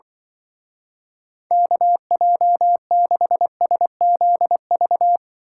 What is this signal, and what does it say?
Morse code, heard as a keyed tone, about 24 wpm, 700 Hz. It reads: KJ6SZV
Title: Speed Racing - USA 2x3 Call Signs 20wpm